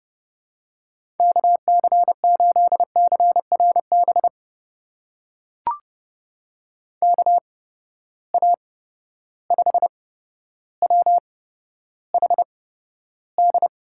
Morse code, heard as a keyed tone, about 31 wpm, spaced as for 13 wpm, 700 Hz. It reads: KC8CRB KA5WHD
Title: USA Call Signs - 6 Characters - 8x Character Spacing - 30wpm